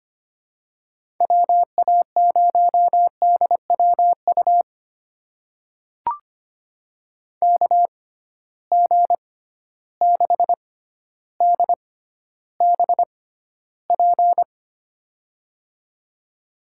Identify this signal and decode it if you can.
Morse code, heard as a keyed tone, about 25 wpm, spaced as for 13 wpm, 700 Hz. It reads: WA0DWU KG6DBP
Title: USA Call Signs - 6 Characters - 6x Character Spacing - 25wpm